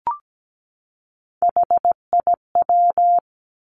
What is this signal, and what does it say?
Morse code, heard as a keyed tone, about 17 wpm, 700 Hz. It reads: HIW
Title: Three Challenging Letter-Number Combinations 17wpm